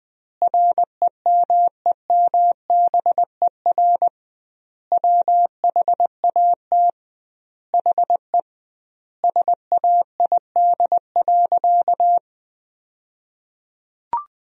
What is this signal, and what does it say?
Morse code, heard as a keyed tone, about 21 wpm, 700 Hz. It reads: REMEMBER WHAT HE SAID.
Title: 2x Word-Spacing - Sentences from Top 400 Words - 20wpm